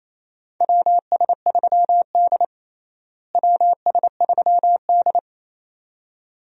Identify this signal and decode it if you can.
Morse code, heard as a keyed tone, about 27 wpm, 700 Hz. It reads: WS3D WS3D
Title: USA Call Signs - 4 Characters - Repeated 2x - 28wpm